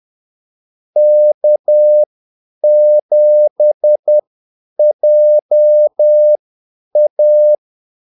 Morse code, heard as a keyed tone, about 10 wpm, spaced as for 8 wpm, 600 Hz. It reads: K7JA
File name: Callsigns-10-8wpm